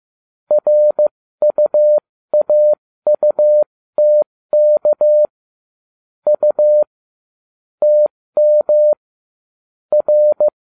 Morse code, heard as a keyed tone, about 15 wpm, spaced as for 13 wpm, 600 Hz. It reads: RUAUTK U TM R